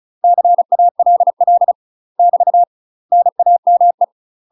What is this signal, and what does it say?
Morse code, heard as a keyed tone, about 35 wpm, 700 Hz. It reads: CALL <BT> NAME
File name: UK MASTER QSOs TRAINING/35WPM